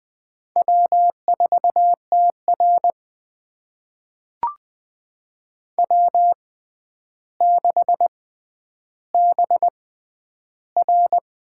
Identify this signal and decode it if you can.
Morse code, heard as a keyed tone, about 20 wpm, spaced as for 10 wpm, 700 Hz. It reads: W4TR W6BR
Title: USA Call Signs - 4 Characters - 6x Character Spacing - 20wpm